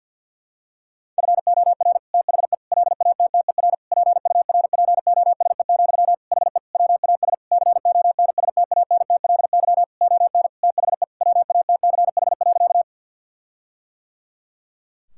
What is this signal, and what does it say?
Morse code, heard as a keyed tone, about 50 wpm, 700 Hz. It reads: FOR THE LATTER PURPOSE, HE WAS CONSTANTLY ON THE WATCH.
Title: Sentences from Top 1600 Words 50wpm